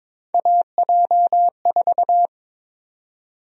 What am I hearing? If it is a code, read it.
Morse code, heard as a keyed tone, about 22 wpm, 700 Hz. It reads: AJ4
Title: USA Call Sign Prefixes - 22wpm